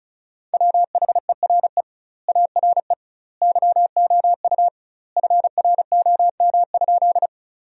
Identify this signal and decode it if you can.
Morse code, heard as a keyed tone, about 36 wpm, 700 Hz. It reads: WHERE ARE YOU FROM?
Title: 2x Word-Spacing - Sentences from Top 200 Words - 35wpm